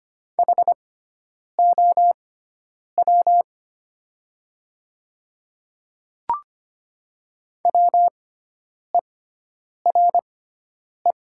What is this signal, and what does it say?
Morse code, heard as a keyed tone, about 25 wpm, spaced as for 9 wpm, 700 Hz. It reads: HOW WERE
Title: 6x Character Spacing - Top 300 Words - 25wpm